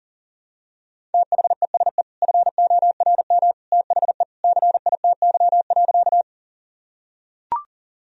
Morse code, heard as a keyed tone, about 37 wpm, 700 Hz. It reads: THESE FORM THE CITY.
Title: Sentences from Top 1000 Words - Review Part 1 - 40wpm